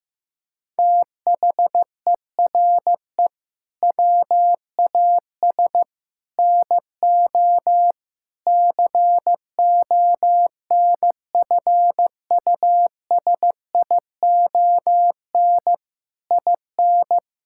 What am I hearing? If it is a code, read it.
Morse code, heard as a keyed tone, about 15 wpm, 700 Hz. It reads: THERE WAS NO CONFUSION IN
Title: Sentences from Top 1600 Words 15wpm